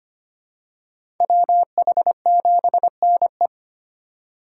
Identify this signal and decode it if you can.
Morse code, heard as a keyed tone, about 25 wpm, 700 Hz. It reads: WH7NE
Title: USA Call Signs - 5 Characters - 4x Character Spacing - 25wpm